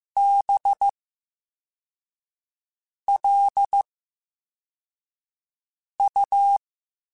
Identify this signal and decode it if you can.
Morse code, heard as a keyed tone, about 15 wpm, spaced as for 4 wpm, 800 Hz. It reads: BLU
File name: Special INGs-mp3s / Special INGs-mp3s / VerySlowToFaster Ing @13